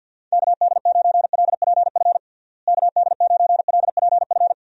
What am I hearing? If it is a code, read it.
Morse code, heard as a keyed tone, about 46 wpm, 700 Hz. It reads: KD9LPF KD9LPF
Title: USA Call Signs - 6 Characters - Repeated 2x - 50wpm